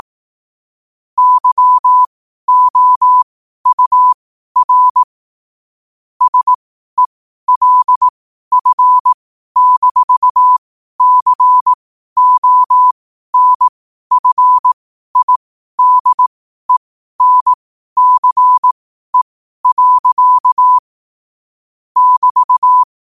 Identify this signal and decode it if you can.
Morse code, heard as a keyed tone, about 18 wpm, spaced as for 12 wpm, 1000 Hz. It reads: YOUR SELF-CONFIDENCE. <BT>
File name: qotdfiles / 12wpm